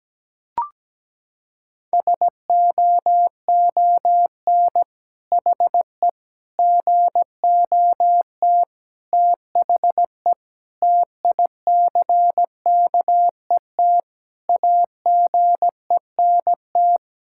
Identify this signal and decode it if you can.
Morse code, heard as a keyed tone, about 17 wpm, 700 Hz. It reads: SOON HE GOT THE TICKET AGENT
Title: Sentences from Top 1600 Words 17wpm